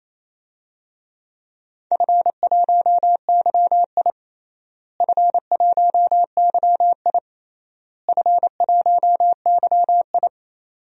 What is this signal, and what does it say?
Morse code, heard as a keyed tone, about 28 wpm, 700 Hz. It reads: F1YI F1YI F1YI
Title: DXCC-100 Call Signs - 4 Characters - Repeated 3x - 28wpm